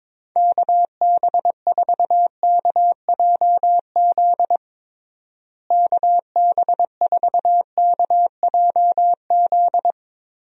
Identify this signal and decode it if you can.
Morse code, heard as a keyed tone, about 23 wpm, 700 Hz. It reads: KB4KJZ KB4KJZ
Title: USA Call Signs - 6 Characters - Repeated 3x - 22wpm